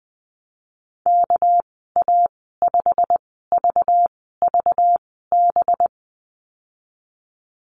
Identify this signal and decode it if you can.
Morse code, heard as a keyed tone, about 20 wpm, spaced as for 15 wpm, 700 Hz. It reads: KA5VVB
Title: 213 Lesson 34g - Number 8 - Call Signs 20wpm